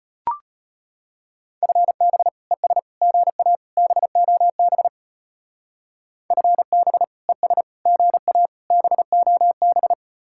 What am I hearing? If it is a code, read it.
Morse code, heard as a keyed tone, about 30 wpm, 700 Hz. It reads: FB ES GA BOB FB ES GA BOB
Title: Speed-Racing Common QSO Phrases Encore - 25wpm